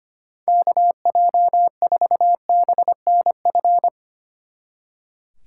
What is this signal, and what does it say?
Morse code, heard as a keyed tone, about 32 wpm, 700 Hz. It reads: KJ4BNF